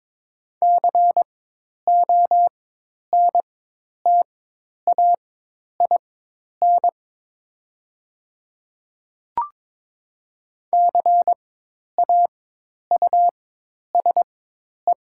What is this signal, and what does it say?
Morse code, heard as a keyed tone, about 22 wpm, spaced as for 10 wpm, 700 Hz. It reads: CONTAIN CAUSE
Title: 4x Character Spacing - Top 500 Words - 22wpm